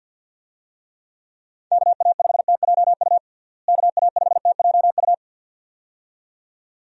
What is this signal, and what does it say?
Morse code, heard as a keyed tone, about 51 wpm, 700 Hz. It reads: KA5TJU KA5TJU
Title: USA Call Signs - 6 Characters - Repeated 2x - 50wpm